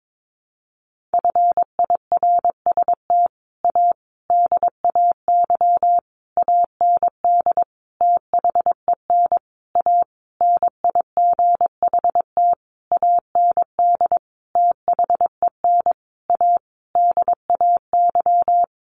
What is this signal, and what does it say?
Morse code, heard as a keyed tone, about 22 wpm, 700 Hz. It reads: FIRST A DAY AND THEN A NIGHT AND THEN A DAY
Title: A Sound Of Thunder 22wpm